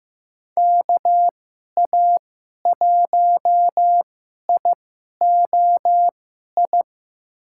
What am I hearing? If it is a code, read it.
Morse code, heard as a keyed tone, about 15 wpm, spaced as for 11 wpm, 700 Hz. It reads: KA1IOI
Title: USA Call Signs - 6 Characters - 2x Character Spacing - 15wpm